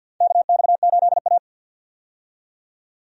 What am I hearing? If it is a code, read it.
Morse code, heard as a keyed tone, about 47 wpm, 700 Hz. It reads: KX8A
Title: USA Call Signs - 4 Characters - 50wpm